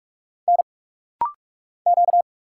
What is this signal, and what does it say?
Morse code, heard as a keyed tone, about 46 wpm, 700 Hz. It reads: N Q
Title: Single Letters - Rapid-Fire 45wpm